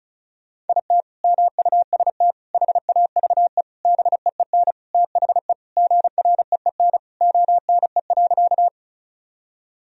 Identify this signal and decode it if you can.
Morse code, heard as a keyed tone, about 34 wpm, 700 Hz. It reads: IT MUST HAVE BEEN THE GREEN ONE.
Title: Sentences from Top 1000 Words - Review Part 2 - 35wpm